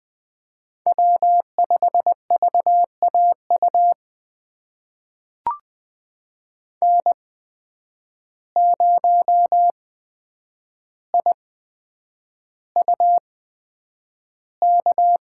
Morse code, heard as a keyed tone, about 20 wpm, spaced as for 9 wpm, 700 Hz. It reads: W5VAU N0IUK
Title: USA Call Signs - 5 Characters - 8x Character Spacing - 20wpm